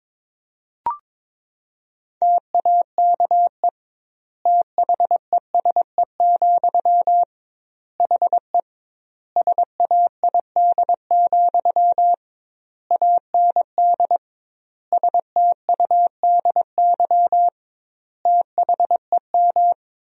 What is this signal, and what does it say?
Morse code, heard as a keyed tone, about 22 wpm, 700 Hz. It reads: TAKE THESE, HE SAID, AND STUDY THEM
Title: 2x Word-Spacing - Sentences from Top 300 Words - 22wpm